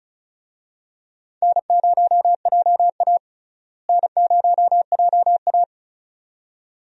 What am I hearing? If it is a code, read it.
Morse code, heard as a keyed tone, about 36 wpm, 700 Hz. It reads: N0JA N0JA
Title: USA Call Signs - 4 Characters - Repeated 2x - 35wpm